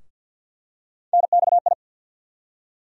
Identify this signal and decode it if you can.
Morse code, heard as a keyed tone, about 48 wpm, 700 Hz. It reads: NXI
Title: USA Call Sign Postfixes - 3 Letters 50wpm